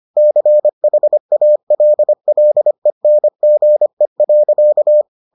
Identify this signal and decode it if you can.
Morse code, heard as a keyed tone, about 25 wpm, 600 Hz. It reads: CHALLENGE.